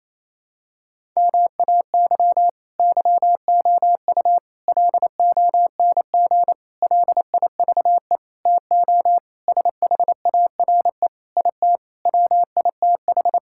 Morse code, heard as a keyed tone, about 28 wpm, 700 Hz. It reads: MAY YOU LONG LIVE TO SHARE IT WITH
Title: Sentences from Top 1000 Words - Review Part 6 - 28wpm